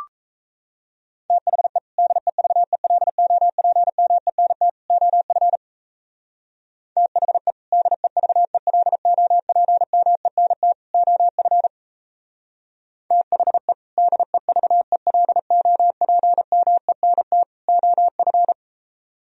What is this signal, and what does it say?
Morse code, heard as a keyed tone, about 33 wpm, 700 Hz. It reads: THE DEVELOPMENT OF THE DEVELOPMENT OF THE DEVELOPMENT OF
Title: Speed-Racing - Sets of 3 Words 28wpm